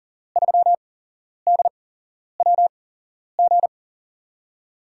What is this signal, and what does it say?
Morse code, heard as a keyed tone, about 37 wpm, spaced as for 11 wpm, 700 Hz. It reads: 3DWG
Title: USA Call Signs - 8x Character Spacing - 40wpm